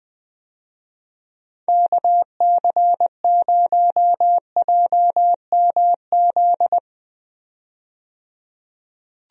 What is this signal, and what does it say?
Morse code, heard as a keyed tone, about 20 wpm, 700 Hz. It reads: KC0JMZ
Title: USA Call Signs - 6 Characters - 20wpm